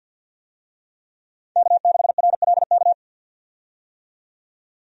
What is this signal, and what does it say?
Morse code, heard as a keyed tone, about 52 wpm, 700 Hz. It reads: K6RLK
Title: USA Call Signs - 5 Characters - 50wpm